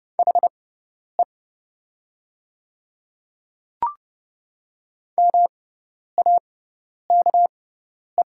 Morse code, heard as a keyed tone, about 30 wpm, spaced as for 10 wpm, 700 Hz. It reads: HE MAKE
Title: 6x Character Spacing - Top 200 Words - 30wpm